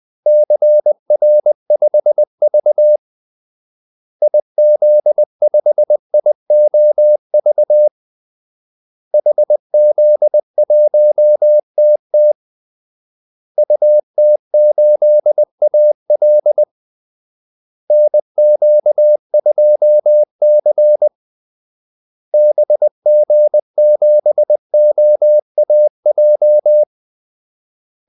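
Morse code, heard as a keyed tone, about 20 wpm, 600 Hz. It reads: CR5V IZ5IOV HZ1TT UT8AL NQ2C BG7OAJ
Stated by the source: Morse call-sign practice at 20 wpm.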